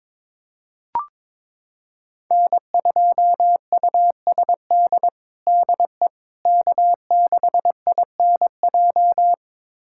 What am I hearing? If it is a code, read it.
Morse code, heard as a keyed tone, about 22 wpm, 700 Hz. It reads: N2USD DE K6INJ
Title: USA Callsign DE Callsign - 5 Characters - 22wpm